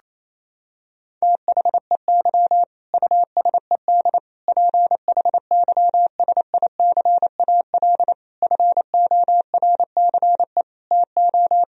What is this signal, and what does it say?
Morse code, heard as a keyed tone, about 28 wpm, 700 Hz. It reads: THEY USED PHYSICAL FORCE TO
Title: Sentences from Top 1000 Words - Review Part 4 - 28wpm